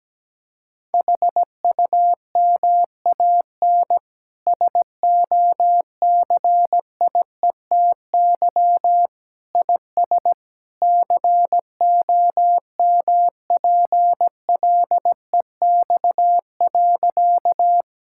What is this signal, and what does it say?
Morse code, heard as a keyed tone, about 17 wpm, 700 Hz. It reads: HUMAN SOCIETY IS COMPLEX.
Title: Sentences from Top 1000 Words - Review Part 6 - 17wpm